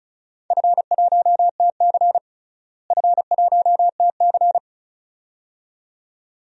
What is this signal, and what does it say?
Morse code, heard as a keyed tone, about 34 wpm, 700 Hz. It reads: F1TC F1TC
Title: DXCC-100 Call Signs - 4 Characters - Repeated 2x - 35wpm